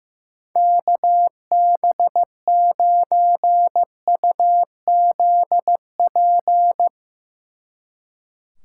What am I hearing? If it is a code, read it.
Morse code, heard as a keyed tone, about 16 wpm, 700 Hz. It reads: KB9UZP
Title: Speed Racing - USA 2x3 Call Signs 15wpm